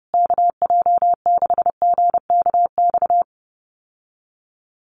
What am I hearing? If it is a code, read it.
Morse code, heard as a keyed tone, about 30 wpm, 700 Hz. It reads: KJ6GKX
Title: USA Call Signs Repeated 3x 30wpm